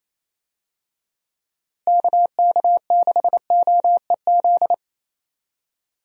Morse code, heard as a keyed tone, about 28 wpm, 700 Hz. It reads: KK6OEZ
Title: USA Call Signs - 6 Characters - Repeated 3x - 28wpm